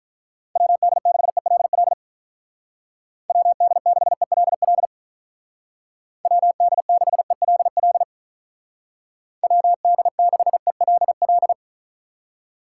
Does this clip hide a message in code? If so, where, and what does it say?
Yes: Morse code from 0.6 to 11.5 s, heard as a keyed tone, about 47 wpm, 700 Hz. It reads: WD6ELL WD6ELL WD6ELL WD6ELL